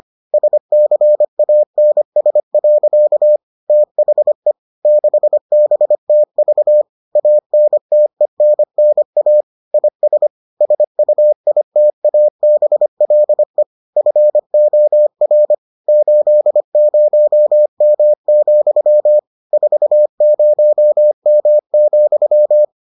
Morse code, heard as a keyed tone, about 25 wpm, 600 Hz. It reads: SCANS. THE 6BTV ANTENNA IS SUITABLE FOR 80M, 40M,